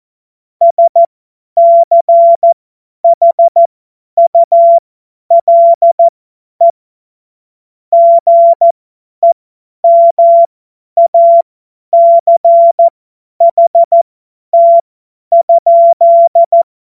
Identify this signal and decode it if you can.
Morse code, heard as a keyed tone, about 14 wpm, spaced as for 10 wpm, 680 Hz. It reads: SCHULE GEMACHT?
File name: Morsekurs / Witze / Witz 21. September 2021, 50 F70